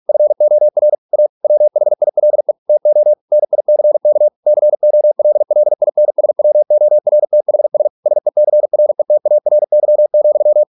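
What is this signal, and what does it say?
Morse code, heard as a keyed tone, about 46 wpm, 600 Hz. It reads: FOR A WHILE TO NICK COLLINSWORTHS SECRETARY,
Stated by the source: Morse practice at 45 wpm, the novel Never by Ken Follett in English, from CW med Gnister.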